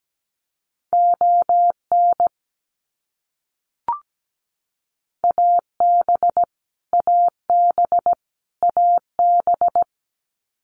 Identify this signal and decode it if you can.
Morse code, heard as a keyed tone, about 17 wpm, 700 Hz. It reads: ON AB AB AB
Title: Canadian Provinces and Territories - Repeated 3x - 17wpm